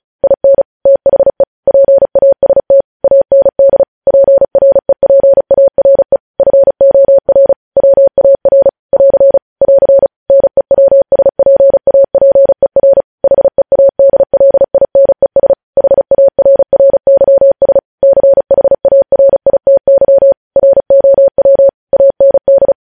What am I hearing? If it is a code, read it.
Morse code, heard as a keyed tone, about 36 wpm, 550 Hz. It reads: IN THE PAST AND PREPARE FOR WAR <AR> <AR> NEWSPAPER HEADLINES HARRYS CHARITY ROW AND
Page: News headlines in Morse code at 35 wpm